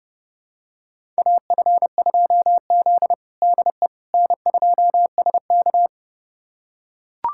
Morse code, heard as a keyed tone, about 31 wpm, 700 Hz. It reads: AF2Z DE N2SK
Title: USA Callsign DE Callsign - 4 Characters - 30wpm